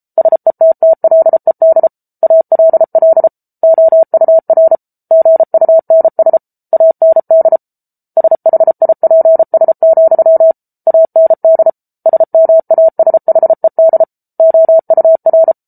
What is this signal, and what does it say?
Morse code, heard as a keyed tone, about 34 wpm, 650 Hz. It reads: SETTLED ALL OUR GUNS AND SHIPS, AND SMASHED OUR